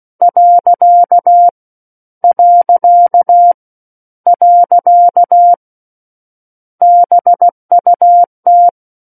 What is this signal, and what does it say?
Morse code, heard as a keyed tone, about 16 wpm, 700 Hz. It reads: . . BUT
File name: Book One / 07-Chapter 7 - 16 WPM